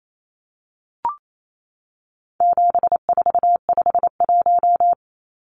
Morse code, heard as a keyed tone, about 28 wpm, 700 Hz. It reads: 7451
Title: Numbers - 4-Digit 28wpm